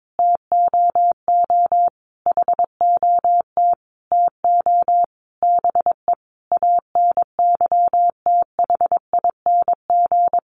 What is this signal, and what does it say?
Morse code, heard as a keyed tone, about 22 wpm, 700 Hz. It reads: TOO HOT TO BE ANYTHING
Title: Sentences from Top 1200 Words 22wpm